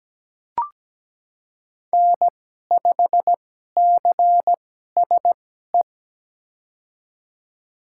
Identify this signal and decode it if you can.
Morse code, heard as a keyed tone, about 17 wpm, spaced as for 12 wpm, 700 Hz. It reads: N5CSE
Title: USA Call Signs - 5 Characters - 2x Character Spacing - 17wpm